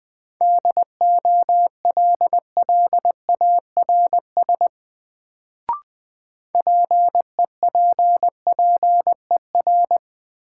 Morse code, heard as a keyed tone, about 20 wpm, 700 Hz. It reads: DOLLARS PEPPER
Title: Sets of 5 Words - New Word Drill - 20wpm